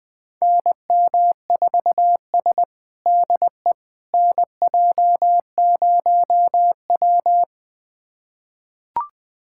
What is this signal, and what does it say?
Morse code, heard as a keyed tone, about 21 wpm, 700 Hz. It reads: NM4S DE NJ0W
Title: USA Callsign DE Callsign - 4 Characters - 20wpm